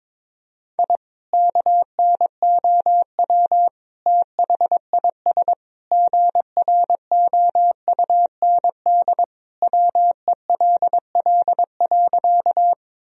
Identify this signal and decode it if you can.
Morse code, heard as a keyed tone, about 22 wpm, 700 Hz. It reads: I KNOW THIS GROUND WELL.
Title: Sentences from Top 1000 Words - Review Part 2 - 22wpm